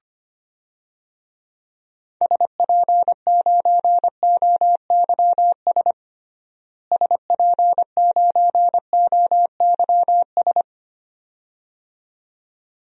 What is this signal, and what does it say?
Morse code, heard as a keyed tone, about 25 wpm, 700 Hz. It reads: SP9OYS SP9OYS
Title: DXCC-100 Call Signs - 6 Characters - Repeated 2x - 25wpm